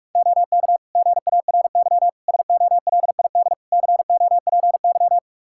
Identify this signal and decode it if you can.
Morse code, heard as a keyed tone, about 43 wpm, 700 Hz. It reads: OK GARY SOLID COPY
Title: Practice QSO - Exchange 3 - More Info - 45wpm